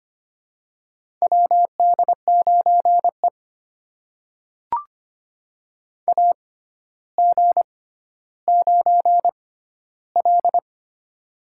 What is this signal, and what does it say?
Morse code, heard as a keyed tone, about 25 wpm, spaced as for 13 wpm, 700 Hz. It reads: WD9E AG9L
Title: USA Call Signs - 4 Characters - 6x Character Spacing - 25wpm